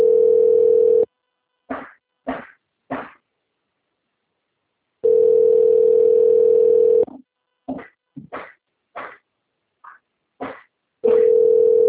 Ringback tone (North America), 440 + 480 Hz.
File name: out- - - - - .19934